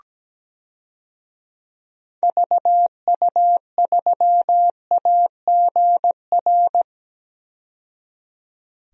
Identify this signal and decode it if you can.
Morse code, heard as a keyed tone, about 17 wpm, 700 Hz. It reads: VU3AGR
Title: DXCC-100 Call Signs - 6 Characters - 17wpm